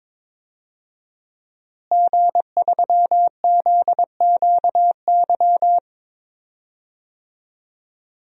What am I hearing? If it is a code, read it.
Morse code, heard as a keyed tone, about 21 wpm, 700 Hz. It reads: G3ZQY
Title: DXCC-100 Call Signs - 5 Characters - Repeated 3x - 22wpm